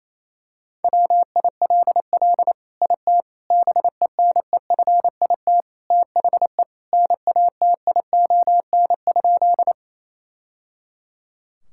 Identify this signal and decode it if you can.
Morse code, heard as a keyed tone, about 28 wpm, 690 Hz. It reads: WILL IT BENEFIT THE NATION?
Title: Sentences from Top 1000 Words - Review Part 6 - 28wpm